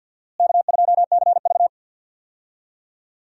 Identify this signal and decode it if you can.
Morse code, heard as a keyed tone, about 51 wpm, 700 Hz. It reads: K2CV
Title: USA Call Signs - 4 Characters - Repeated 3x - 50wpm